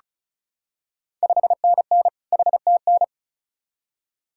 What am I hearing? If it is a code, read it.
Morse code, heard as a keyed tone, about 36 wpm, 700 Hz. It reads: 5NN HTN
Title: Ontario QSO Party Exchange 35wpm